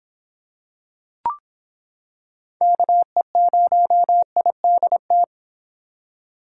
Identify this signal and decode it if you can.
Morse code, heard as a keyed tone, about 26 wpm, 700 Hz. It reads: KE0IDT